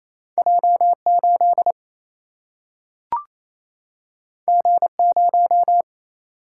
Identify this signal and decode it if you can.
Morse code, heard as a keyed tone, about 28 wpm, 700 Hz. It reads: J 8 G 0
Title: Single-Letter Single-Number 28wpm